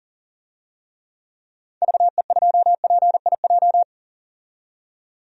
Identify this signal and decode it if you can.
Morse code, heard as a keyed tone, about 40 wpm, 700 Hz. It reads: VE2PIJ
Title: DXCC-100 Call Signs - 6 Characters - 40wpm